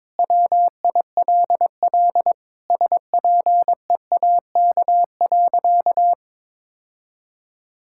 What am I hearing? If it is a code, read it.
Morse code, heard as a keyed tone, about 22 wpm, 700 Hz. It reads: WILL SPEAK.
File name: practice / 22wpm / Sentences with 2-Word Phrases - 22wpm